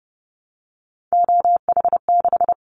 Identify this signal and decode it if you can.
Morse code, heard as a keyed tone, about 30 wpm, 700 Hz. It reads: OH6
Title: Callsign Prefixes for Top 25 DXCC - 30wpm